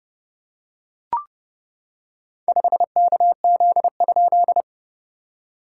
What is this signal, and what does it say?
Morse code, heard as a keyed tone, about 30 wpm, 690 Hz. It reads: 5KZ?
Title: SOTA QSO Practice 30wpm